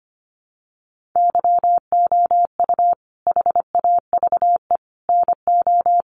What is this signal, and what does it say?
Morse code, heard as a keyed tone, about 25 wpm, 700 Hz. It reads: YOU HAVE NO
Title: Sentences from Top 1500 Words 25wpm